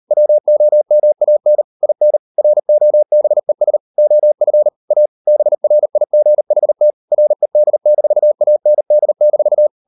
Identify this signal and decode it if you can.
Morse code, heard as a keyed tone, about 40 wpm, 600 Hz. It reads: WOMAN IN ROBES OF A BRIGHT RED-AND-